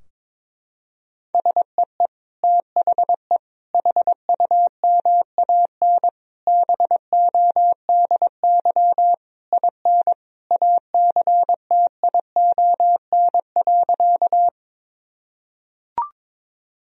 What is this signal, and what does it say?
Morse code, heard as a keyed tone, about 22 wpm, 700 Hz. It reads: SEE THE HUMAN BODY IN ACTION.
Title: Sentences from Top 1000 Words - Review Part 4 - 22wpm